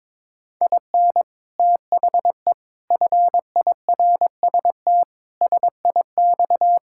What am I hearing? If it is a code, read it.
Morse code, heard as a keyed tone, about 22 wpm, 700 Hz. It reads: IN THE FIRST SIX
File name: practice / 22wpm / Sets of 8 Words - 22wpm